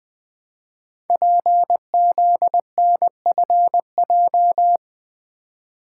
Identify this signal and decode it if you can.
Morse code, heard as a keyed tone, about 20 wpm, 700 Hz. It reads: PZNFJ
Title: Code Groups - 5 Letters - 20wpm